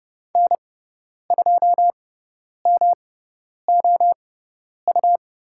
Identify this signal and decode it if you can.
Morse code, heard as a keyed tone, about 30 wpm, spaced as for 10 wpm, 700 Hz. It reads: N2MOU
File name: Lesson 18g - Letter m - Call Signs 30wpm